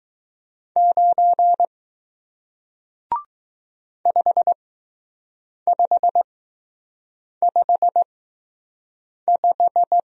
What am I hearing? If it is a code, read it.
Morse code, heard as a keyed tone, about 20 wpm, spaced as for 6 wpm, 700 Hz. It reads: 9 5555